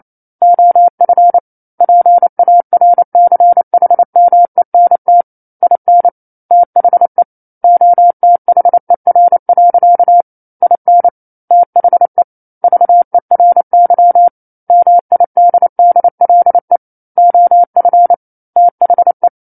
Morse code, heard as a keyed tone, about 28 wpm, 700 Hz. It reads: OF PARCHMENT IN THE OTHER. IN THE VERY MIDDLE OF THE
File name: 09-Chapter-09 28 WPM